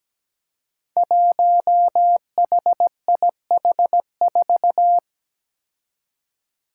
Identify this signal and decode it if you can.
Morse code, heard as a keyed tone, about 17 wpm, 700 Hz. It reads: 1HIH4
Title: Five Challenging Letter-Number Combinations 17wpm